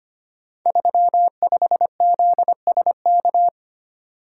Morse code, heard as a keyed tone, about 25 wpm, 700 Hz. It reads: 35ZSK
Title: Code Groups - 5 Letter:Numbers - 25wpm